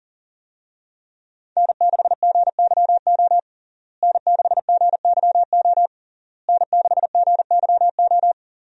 Morse code, heard as a keyed tone, about 40 wpm, 700 Hz. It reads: N6GYO N6GYO N6GYO
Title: USA Call Signs - 5 Characters - Repeated 3x - 40wpm